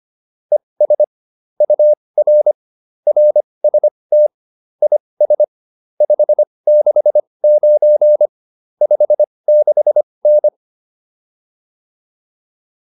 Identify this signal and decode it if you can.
Morse code, heard as a keyed tone, about 25 wpm, spaced as for 20 wpm, 600 Hz. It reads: ES UR RST IS 569 56N